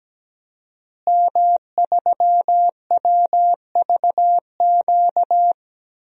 Morse code, heard as a keyed tone, about 17 wpm, 700 Hz. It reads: M3WVQ
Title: DXCC-100 Call Signs - 5 Characters - Repeated 2x - 17wpm